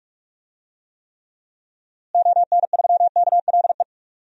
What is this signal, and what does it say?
Morse code, heard as a keyed tone, about 46 wpm, 700 Hz. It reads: ON3KLE
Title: DXCC-100 Call Signs - 6 Characters - Repeated 2x - 45wpm